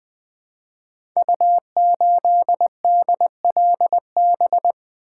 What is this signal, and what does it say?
Morse code, heard as a keyed tone, about 20 wpm, 700 Hz. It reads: U8DLB
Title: Code Groups - 5 Letter:Numbers - 20wpm